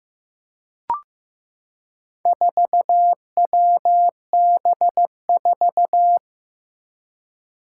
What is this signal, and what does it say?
Morse code, heard as a keyed tone, about 15 wpm, 700 Hz. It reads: 4WB4
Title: Four Challenging Letter-Number Combinations 15wpm